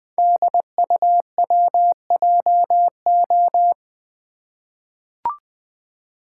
Morse code, heard as a keyed tone, about 20 wpm, 700 Hz. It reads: DUWJO
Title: Code Groups - 5 Letter:Numbers - 20wpm